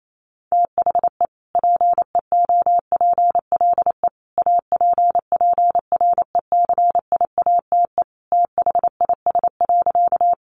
Morse code, heard as a keyed tone, about 28 wpm, 700 Hz. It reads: THE PEOPLE APPRECIATE THIS.
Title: Sentences from Top 1300 Words 28wpm